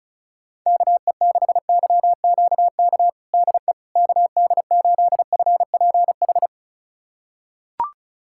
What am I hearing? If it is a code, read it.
Morse code, heard as a keyed tone, about 36 wpm, 700 Hz. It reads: KE6YQK DE KD8FPH
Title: USA Callsign DE Callsign 35wpm